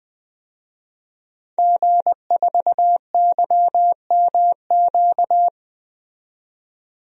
Morse code, heard as a keyed tone, about 20 wpm, 700 Hz. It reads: G4YMQ
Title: DXCC-100 Call Signs - 5 Characters - 20wpm